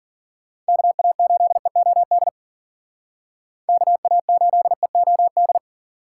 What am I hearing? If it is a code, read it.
Morse code, heard as a keyed tone, about 52 wpm, 700 Hz. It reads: KA8EOD KA8EOD